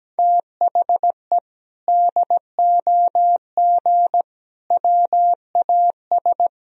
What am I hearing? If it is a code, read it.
Morse code, heard as a keyed tone, about 17 wpm, 700 Hz. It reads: THE DOG WAS
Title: Sentences from Top 1000 Words - Review Part 2 - 17wpm